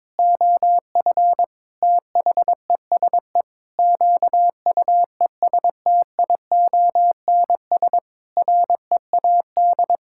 Morse code, heard as a keyed tone, about 22 wpm, 700 Hz. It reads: OF THESE QUESTIONS READ
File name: Sentences with 9-Word Phrases - 22wpm